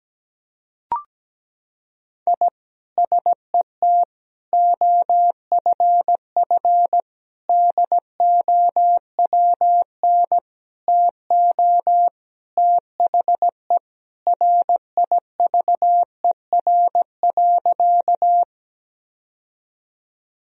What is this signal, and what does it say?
Morse code, heard as a keyed tone, about 17 wpm, 700 Hz. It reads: I SET OFF DOWN TO THE RIVER.
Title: Sentences from Top 1000 Words - Review Part 1 - 17wpm